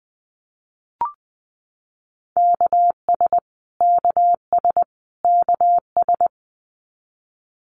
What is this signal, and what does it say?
Morse code, heard as a keyed tone, about 20 wpm, 700 Hz. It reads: KS KS KS
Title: US States Repeated 3x 20wpm